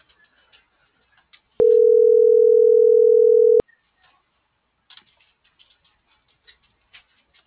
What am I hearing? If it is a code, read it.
Ringback tone (North America), 440 + 480 Hz.